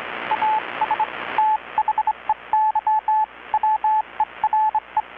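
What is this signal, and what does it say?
Morse code, heard as a keyed tone, about 25 wpm, spaced as for 22 wpm, 860 Hz, with radio noise. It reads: AS THEY WERE
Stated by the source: Morse from KPH's Night of Nights XX maritime radiotelegraphy event, received in Chiapas, Mexico about 3,800 km away on a Kenwood R-600 with a grounded 10 m long-wire antenna.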